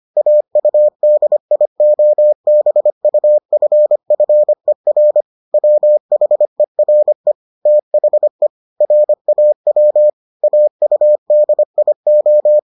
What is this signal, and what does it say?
Morse code, heard as a keyed tone, about 25 wpm, 600 Hz. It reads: AUDIOBUFFER WHERE THE RAW AUDIO